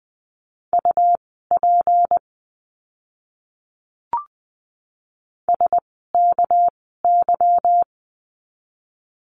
Morse code, heard as a keyed tone, about 20 wpm, spaced as for 15 wpm, 700 Hz. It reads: UP SKY
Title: 199 Lesson 32f - Letter k - Words 20wpm